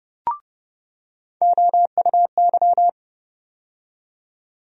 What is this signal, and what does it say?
Morse code, heard as a keyed tone, about 30 wpm, 700 Hz. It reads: OUY
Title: USA Call Sign Postfixes - 3 Letters 30wpm